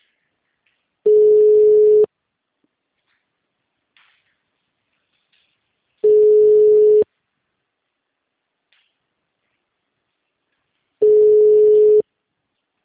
Ringback tone (Europe, 425 Hz).
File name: out- -2003- - - .40721